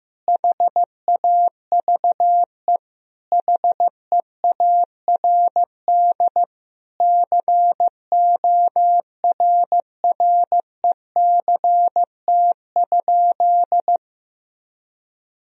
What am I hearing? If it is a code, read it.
Morse code, heard as a keyed tone, about 15 wpm, 700 Hz. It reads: HAVE HEARD CORRECT?
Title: Sentences from Top 1000 Words - Review Part 2 - 15wpm